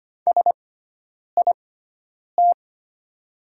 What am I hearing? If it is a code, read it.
Morse code, heard as a keyed tone, about 25 wpm, spaced as for 9 wpm, 700 Hz. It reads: SIT
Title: 6x Character Spacing - Top 500 Words - 25wpm